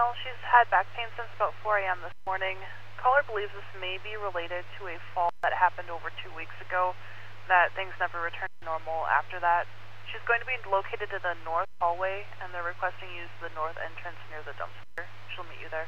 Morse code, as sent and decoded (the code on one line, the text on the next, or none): none